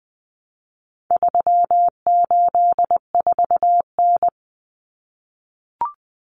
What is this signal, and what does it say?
Morse code, heard as a keyed tone, about 20 wpm, 700 Hz. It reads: 384N
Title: Common Cut Numbers - 4-Digit 20wpm